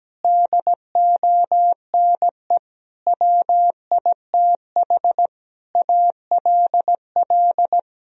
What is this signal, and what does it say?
Morse code, heard as a keyed tone, about 17 wpm, 700 Hz. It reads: DONE WITH ALL
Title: Sentences from Top 1000 Words - Review Part 2 - 17wpm